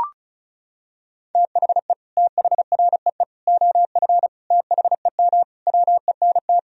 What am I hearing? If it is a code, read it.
Morse code, heard as a keyed tone, about 35 wpm, 700 Hz. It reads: THE THREE OF THEM WENT
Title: Sentences from Top 1000 Words - Review Part 1 - 35wpm